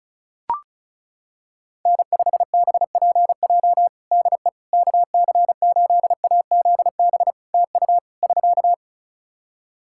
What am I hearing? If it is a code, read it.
Morse code, heard as a keyed tone, about 34 wpm, 700 Hz. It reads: N5BPJ DE KC8AZB TU <SK>
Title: K1USN SST QSO Practice 35wpm